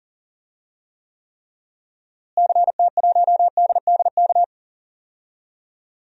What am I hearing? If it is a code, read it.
Morse code, heard as a keyed tone, about 40 wpm, 700 Hz. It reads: CT1DDK
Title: DXCC-100 Call Signs - 6 Characters - Repeated 3x - 40wpm